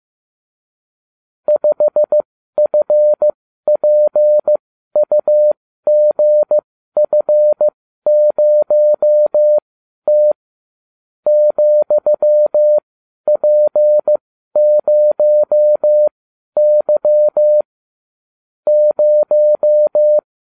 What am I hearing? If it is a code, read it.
Morse code, heard as a keyed tone, about 15 wpm, spaced as for 12 wpm, 600 Hz. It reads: 5FPUGF0T ,P0Y 0